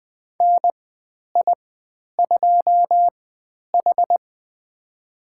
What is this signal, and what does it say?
Morse code, heard as a keyed tone, about 20 wpm, spaced as for 10 wpm, 700 Hz. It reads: NI2H
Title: 095 Lesson 16g - Letter c - Call Signs 20wpm